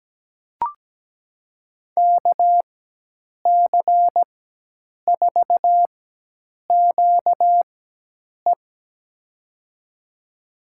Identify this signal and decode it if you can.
Morse code, heard as a keyed tone, about 17 wpm, spaced as for 8 wpm, 700 Hz. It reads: KC4QE
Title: USA Call Signs - 5 Characters - 4x Character Spacing - 17wpm